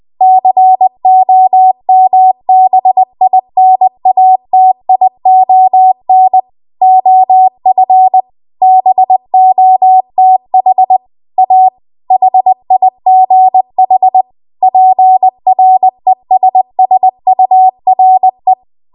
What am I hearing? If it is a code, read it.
Morse code, heard as a keyed tone, about 20 wpm, 750 Hz. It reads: COMBINATION OF BOTH A HIGH PRESSURE
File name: Archive / 20 WPM / 20WPM